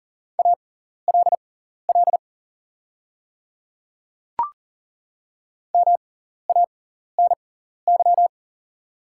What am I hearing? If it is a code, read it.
Morse code, heard as a keyed tone, about 40 wpm, spaced as for 14 wpm, 700 Hz. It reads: ALL MANY
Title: 6x Character Spacing - Top 100 Words - 40wpm